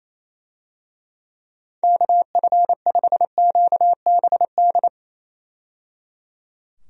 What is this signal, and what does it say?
Morse code, heard as a keyed tone, about 27 wpm, 700 Hz. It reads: KF5QBD
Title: USA Call Signs - 6 Characters - 28wpm